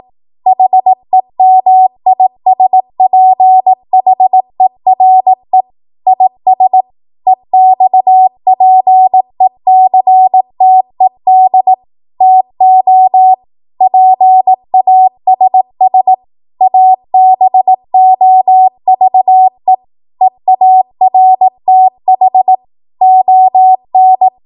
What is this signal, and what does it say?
Morse code, heard as a keyed tone, about 18 wpm, 750 Hz. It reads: HEMISPHERE IS EXPECTED TO PASS ABOVE EARTH ON
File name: Morse / 18WPM